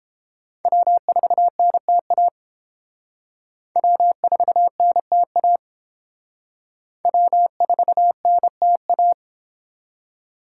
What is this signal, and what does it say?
Morse code, heard as a keyed tone, about 29 wpm, 700 Hz. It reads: W4NTA W4NTA W4NTA